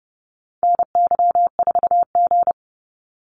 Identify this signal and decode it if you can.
Morse code, heard as a keyed tone, about 30 wpm, 700 Hz. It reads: NY4G
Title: NA SOTA Activator Callsigns 30wpm